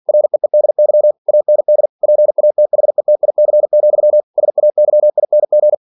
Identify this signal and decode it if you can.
Morse code, heard as a keyed tone, about 46 wpm, 600 Hz. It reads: REEDY AND PATHETIC, SAYING
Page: Never 47wpm